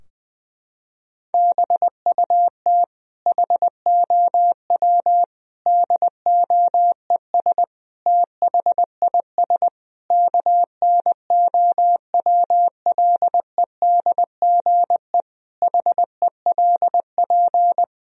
Morse code, heard as a keyed tone, about 20 wpm, 700 Hz. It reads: BUT HOW DOES THIS KNOWLEDGE HELP
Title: Sentences from Top 1000 Words - Review Part 3 - 20wpm